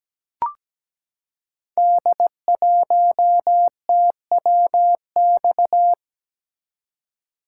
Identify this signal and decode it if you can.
Morse code, heard as a keyed tone, about 17 wpm, 700 Hz. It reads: D1TWX
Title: Code Groups - 5 Letter:Numbers - 17wpm